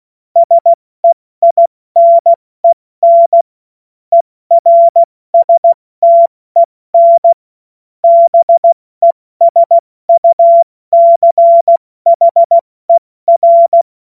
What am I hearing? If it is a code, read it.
Morse code, heard as a keyed tone, about 16 wpm, 680 Hz. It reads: SEINEN ERSTEN BESUCHER
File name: Morsekurs / Witze / Witz 10. Mai 2022 70 F80